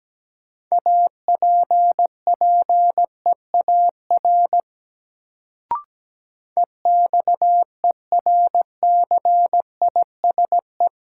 Morse code, heard as a keyed tone, about 17 wpm, 700 Hz. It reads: APPEAR EXERCISE
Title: Sentences from Top 1200 Words - New Word Drill Alt - 17wpm